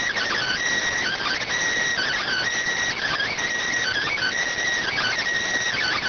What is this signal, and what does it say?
Emergency Alert System (SAME) tones (header bursts), heard as harsh buzzing bursts. It is heard from 0.8 to 5.5 s.